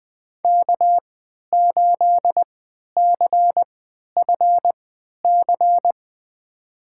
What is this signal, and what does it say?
Morse code, heard as a keyed tone, about 20 wpm, spaced as for 11 wpm, 700 Hz. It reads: K8CFC
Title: USA Call Signs - 5 Characters - 3x Character Spacing - 20wpm